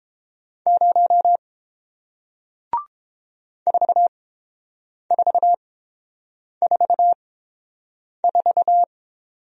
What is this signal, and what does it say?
Morse code, heard as a keyed tone, about 30 wpm, spaced as for 7 wpm, 700 Hz. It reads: 0 4444